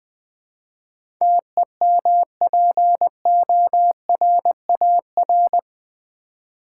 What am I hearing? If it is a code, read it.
Morse code, heard as a keyed tone, about 20 wpm, 700 Hz. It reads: TEMPORAR
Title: Build A Word - 4 Letters At A Time - 11-Letter Words - 20wpm